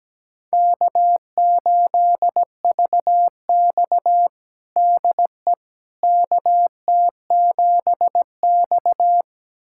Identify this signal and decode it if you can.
Morse code, heard as a keyed tone, about 17 wpm, 700 Hz. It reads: K8VX DE KT7X
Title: USA Callsign DE Callsign - 4 Characters - 17wpm